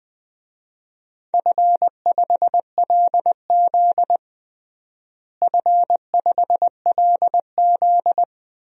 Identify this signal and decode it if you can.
Morse code, heard as a keyed tone, about 21 wpm, 700 Hz. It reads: F5LZ F5LZ
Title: DXCC-100 Call Signs - 4 Characters - Repeated 2x - 20wpm